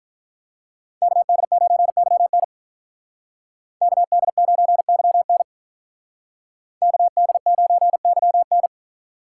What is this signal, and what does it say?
Morse code, heard as a keyed tone, about 46 wpm, 700 Hz. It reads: KD9YN KD9YN KD9YN